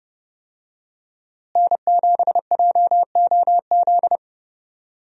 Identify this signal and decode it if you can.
Morse code, heard as a keyed tone, about 30 wpm, 700 Hz. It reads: N7JOZ
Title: USA Call Signs - 5 Characters - Repeated 2x - 30wpm